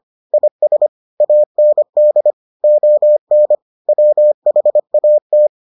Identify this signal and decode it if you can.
Morse code, heard as a keyed tone, about 25 wpm, 600 Hz. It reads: IS AND ON WHAT